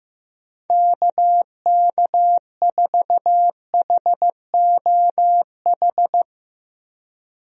Morse code, heard as a keyed tone, about 15 wpm, 700 Hz. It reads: KK4HOH
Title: USA Call Signs - 6 Characters - 15wpm